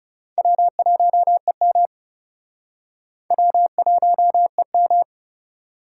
Morse code, heard as a keyed tone, about 40 wpm, 700 Hz. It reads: W1EM W1EM